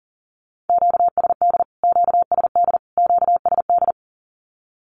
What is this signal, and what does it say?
Morse code, heard as a keyed tone, about 40 wpm, 700 Hz. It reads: QSD QSD QSD
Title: Q-Codes Repeated 3x 40wpm